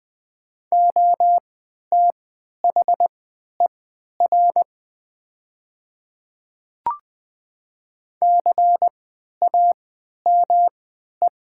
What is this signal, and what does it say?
Morse code, heard as a keyed tone, about 20 wpm, spaced as for 11 wpm, 700 Hz. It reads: OTHER CAME
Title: 3x Character Spacing - Top 200 Words - 20wpm